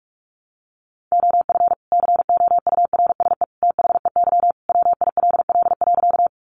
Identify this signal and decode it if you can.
Morse code, heard as a keyed tone, about 44 wpm, 700 Hz. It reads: OF COURSE THEY WILL.
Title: Sentences from Top 500 Words 45wpm